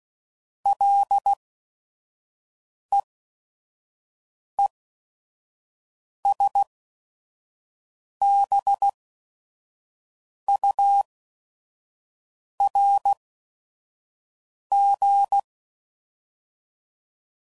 Morse code, heard as a keyed tone, about 16 wpm, spaced as for 5 wpm, 800 Hz. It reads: LEESBURG
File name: CW Code Groups @5WPM / fromSM's / Cities